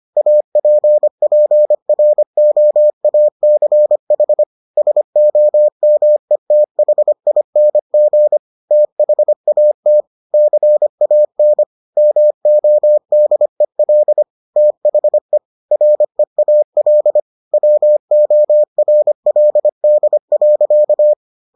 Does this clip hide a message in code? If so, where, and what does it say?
Yes: Morse code from 0.2 to 21.1 s, heard as a keyed tone, about 25 wpm, 600 Hz. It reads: APPROACH SOMETHING THAT CAN MODEL THE REAL WORLD.